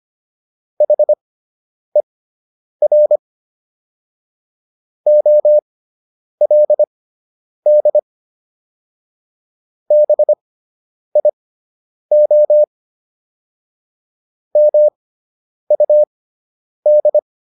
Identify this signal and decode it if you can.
Morse code, heard as a keyed tone, about 25 wpm, spaced as for 9 wpm, 600 Hz. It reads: HER OLD BIO MUD